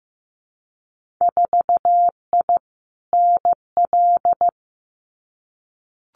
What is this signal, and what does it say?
Morse code, heard as a keyed tone, about 15 wpm, 700 Hz. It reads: 4I NL
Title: Winter Field Day Exchanges Encore 15wpm